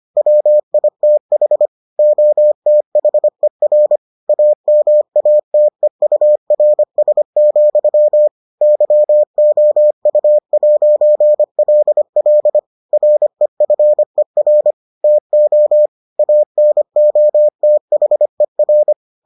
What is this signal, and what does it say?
Morse code, heard as a keyed tone, about 25 wpm, 600 Hz. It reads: WITH OTHER AMATEURS, YOU'LL REFER TO ANOTHER